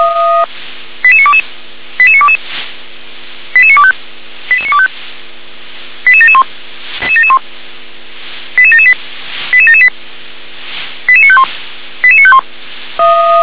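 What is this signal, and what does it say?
Five-tone selective call (ZVEI), heard as quick runs of five tones, tones of about 70 ms. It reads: 88922 88922 88925 88925 88771 88778 88778 88951 88951
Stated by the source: Five-tone ZVEI selective call 88952 received on an SDR.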